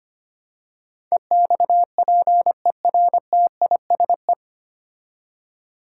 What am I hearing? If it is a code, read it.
Morse code, heard as a keyed tone, about 25 wpm, 700 Hz. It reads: EXPERTISE
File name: 9 Letter Words - First 3 Letters Given - 25wpm